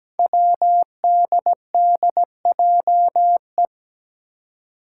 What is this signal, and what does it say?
Morse code, heard as a keyed tone, about 17 wpm, 700 Hz. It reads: WDDJE
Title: Five Challenging Letter-Number Combinations 17wpm